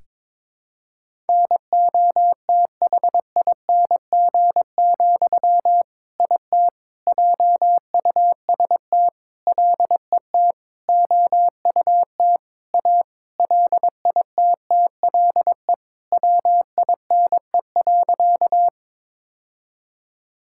Morse code, heard as a keyed tone, about 22 wpm, 700 Hz. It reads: NOTHING, IT JUST LET OUT A LITTLE WINE.
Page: morse-code-ninja-practice-22wpm